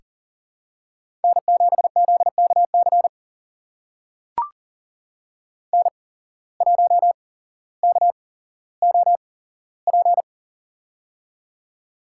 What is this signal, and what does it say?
Morse code, heard as a keyed tone, about 40 wpm, spaced as for 17 wpm, 700 Hz. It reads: N7ZKC N1KOP
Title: USA Call Signs - 5 Characters - 8x Character Spacing - 40wpm